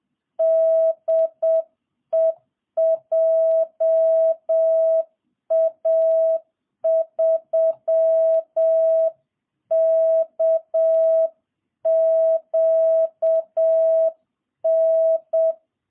Morse code, heard as a keyed tone, about 7 wpm, 650 Hz. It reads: DE JA3KQN